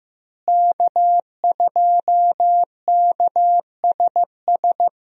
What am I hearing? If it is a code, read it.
Morse code, heard as a keyed tone, about 16 wpm, 700 Hz. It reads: K2KSS
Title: USA Call Signs - 5 Characters - Repeated 2x - 15wpm